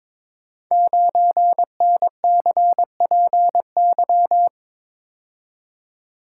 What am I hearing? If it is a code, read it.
Morse code, heard as a keyed tone, about 22 wpm, 700 Hz. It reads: 9NCPY
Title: Code Groups - 5 Letter:Numbers - 22wpm